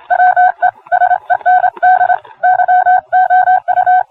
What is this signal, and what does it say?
Morse code, heard as a keyed tone, about 26 wpm, 730 Hz. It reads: WE SEND YOU